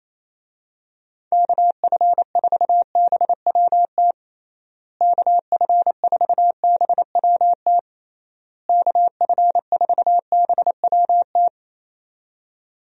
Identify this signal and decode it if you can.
Morse code, heard as a keyed tone, about 27 wpm, 700 Hz. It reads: KF4BWT KF4BWT KF4BWT
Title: USA Call Signs - 6 Characters - Repeated 3x - 28wpm